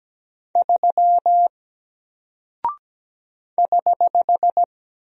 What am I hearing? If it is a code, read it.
Morse code, heard as a keyed tone, about 17 wpm, 700 Hz. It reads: 3 <HH>
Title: Single Character (Letter:Number:Punctuation:Prosign) Alt 17wpm